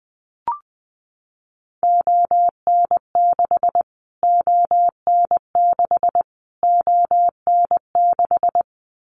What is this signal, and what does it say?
Morse code, heard as a keyed tone, about 20 wpm, 700 Hz. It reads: ON6 ON6 ON6
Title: Callsign Prefixes for Top 25 DXCC - Repeated 3x - 20wpm